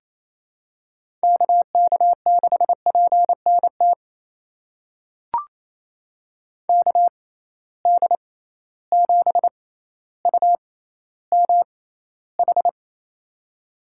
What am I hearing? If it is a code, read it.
Morse code, heard as a keyed tone, about 28 wpm, spaced as for 14 wpm, 700 Hz. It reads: KK6PNT KD7UMH
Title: USA Call Signs - 6 Characters - 6x Character Spacing - 28wpm